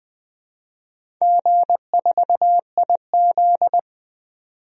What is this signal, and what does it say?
Morse code, heard as a keyed tone, about 20 wpm, 700 Hz. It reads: G4IZ
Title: DXCC-100 Call Signs - 4 Characters - 20wpm